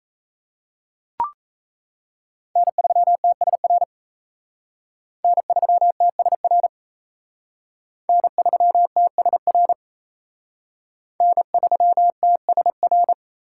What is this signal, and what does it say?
Morse code, heard as a keyed tone, about 35 wpm, 700 Hz. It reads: N3TSR N3TSR N3TSR N3TSR
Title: Speed Racing - USA Call Signs 28wpm